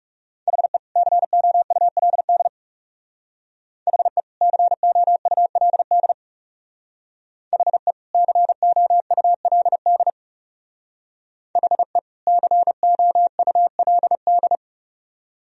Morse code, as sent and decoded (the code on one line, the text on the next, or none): .... / -.-. --- ..- .-.. -.. / .... / -.-. --- ..- .-.. -.. / .... / -.-. --- ..- .-.. -.. / .... / -.-. --- ..- .-.. -..
HE COULD HE COULD HE COULD HE COULD